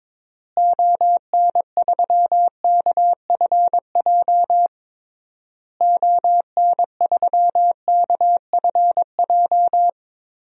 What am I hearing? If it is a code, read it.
Morse code, heard as a keyed tone, about 22 wpm, 700 Hz. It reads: ON3KFJ ON3KFJ
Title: DXCC-100 Call Signs - 6 Characters - Repeated 3x - 22wpm